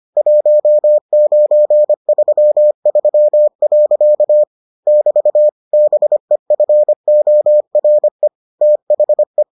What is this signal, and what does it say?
Morse code, heard as a keyed tone, about 25 wpm, 600 Hz. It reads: 1933. <BT> BEFORE THE